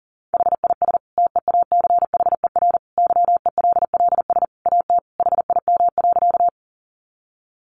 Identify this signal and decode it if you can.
Morse code, heard as a keyed tone, about 40 wpm, 700 Hz. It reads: HIS TEACHER YELLS AT HIM.
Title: Additional Easy Phrases 40wpm